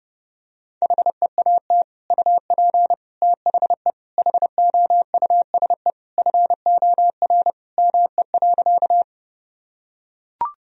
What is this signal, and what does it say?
Morse code, heard as a keyed tone, about 30 wpm, 700 Hz. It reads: HEAT UP THE HOUSE FOR ME.
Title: Sentences from Top 1000 Words - Review Part 3 - 30wpm